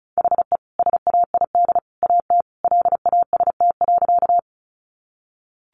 Morse code, heard as a keyed tone, about 34 wpm, 700 Hz. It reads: HE SAID AT LAST.
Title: Sentences from Top 300 Words 35wpm